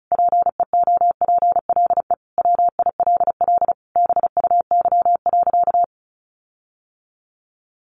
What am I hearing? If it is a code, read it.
Morse code, heard as a keyed tone, about 34 wpm, 700 Hz. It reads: PEOPLE WILL BUY.
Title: Sentences from Top 1300 Words 35wpm